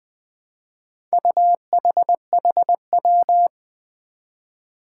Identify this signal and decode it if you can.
Morse code, heard as a keyed tone, about 20 wpm, 700 Hz. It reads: UHHW
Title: Four Challenging Letter-Number Combinations 20wpm